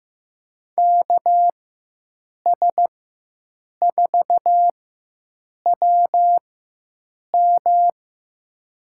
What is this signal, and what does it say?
Morse code, heard as a keyed tone, about 16 wpm, spaced as for 7 wpm, 700 Hz. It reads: KS4WM
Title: USA Call Signs - 5 Characters - 4x Character Spacing - 15wpm